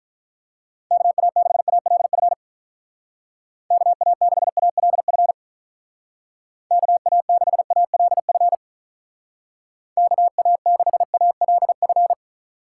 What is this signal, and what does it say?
Morse code, heard as a keyed tone, about 43 wpm, 700 Hz. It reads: KA6ALF KA6ALF KA6ALF KA6ALF
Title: Speed Racing - USA 2x3 Call Signs 35wpm